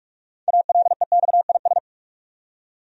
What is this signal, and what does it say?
Morse code, heard as a keyed tone, about 46 wpm, 700 Hz. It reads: ALEXIS
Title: USA Common Names - 45wpm